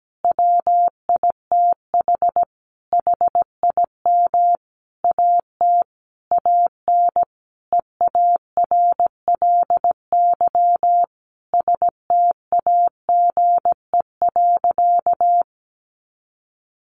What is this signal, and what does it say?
Morse code, heard as a keyed tone, about 17 wpm, 700 Hz. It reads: WITH HIM AT AN EARLY STAGE.
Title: Sentences from Top 1200 Words 17wpm